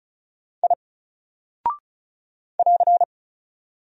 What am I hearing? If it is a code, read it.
Morse code, heard as a keyed tone, about 36 wpm, 700 Hz. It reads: I <AR>
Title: Single Character (Letter:Number:Punctuation:Prosign) 35wpm